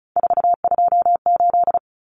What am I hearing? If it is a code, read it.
Morse code, heard as a keyed tone, about 34 wpm, 700 Hz. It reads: 428
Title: Common Cut Numbers - 3-Digit 35wpm